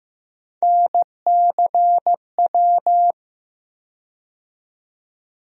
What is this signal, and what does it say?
Morse code, heard as a keyed tone, about 15 wpm, 700 Hz. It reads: NCW
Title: USA Call Sign Postfixes - 3 Letters 15wpm